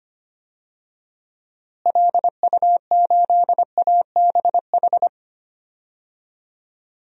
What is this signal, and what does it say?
Morse code, heard as a keyed tone, about 25 wpm, 700 Hz. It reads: LU8ABH
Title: DXCC-100 Call Signs - 6 Characters - Repeated 2x - 25wpm